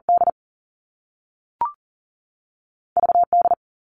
Morse code, heard as a keyed tone, about 40 wpm, 700 Hz. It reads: D V D
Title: Two Challenging Letter-Number Combinations 40wpm